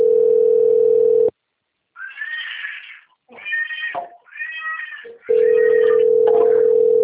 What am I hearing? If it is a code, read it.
Ringback tone (North America), 440 + 480 Hz.
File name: out- - - - - .78797